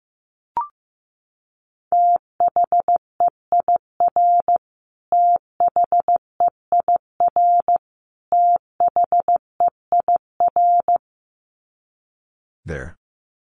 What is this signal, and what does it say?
Morse code, heard as a keyed tone, about 15 wpm, 700 Hz. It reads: THEIR THEIR THEIR
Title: Top 300 Words Repeated 15wpm